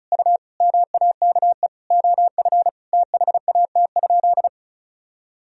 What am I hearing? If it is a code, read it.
Morse code, heard as a keyed tone, about 34 wpm, 700 Hz. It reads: U MAKE OF THAT?
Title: Sentences from Top 100 Words and Abbreviations 35wpm